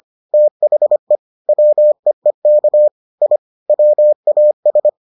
Morse code, heard as a keyed tone, about 25 wpm, 600 Hz. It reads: THE WEEK I WAS